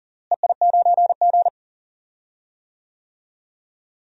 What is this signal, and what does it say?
Morse code, heard as a keyed tone, about 40 wpm, 700 Hz. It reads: EI9G
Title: DXCC-100 Call Signs - 4 Characters - Repeated 2x - 40wpm